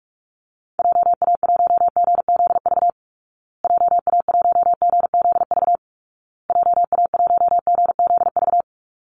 Morse code, heard as a keyed tone, about 44 wpm, 700 Hz. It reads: JA1GZV JA1GZV JA1GZV
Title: Call Signs Repeated 2x - 45wpm